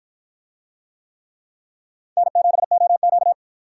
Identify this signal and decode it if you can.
Morse code, heard as a keyed tone, about 52 wpm, 700 Hz. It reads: N7OQ